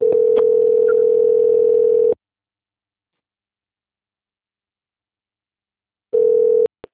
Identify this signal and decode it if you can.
Ringback tone (North America), 440 + 480 Hz.